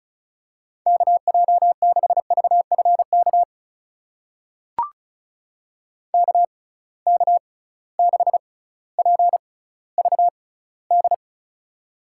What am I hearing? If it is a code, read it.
Morse code, heard as a keyed tone, about 34 wpm, spaced as for 18 wpm, 700 Hz. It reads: KJ6VFK KK6PVD
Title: USA Call Signs - 6 Characters - 6x Character Spacing - 35wpm